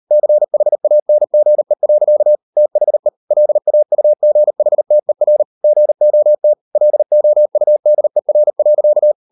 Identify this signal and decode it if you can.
Morse code, heard as a keyed tone, about 38 wpm, 600 Hz. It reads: CHANGE. THE LAUGHTER GOT LOUDER.